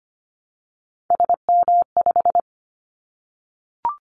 Morse code, heard as a keyed tone, about 25 wpm, 700 Hz. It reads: SM5
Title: Callsign Prefixes for Top 25 DXCC - 25wpm